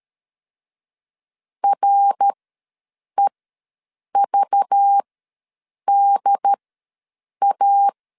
Morse code, heard as a keyed tone, about 13 wpm, spaced as for 7 wpm, 800 Hz. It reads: REVDA
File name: LEZIONI / 5 LBVC / 5 LBV ws 12wpm 3015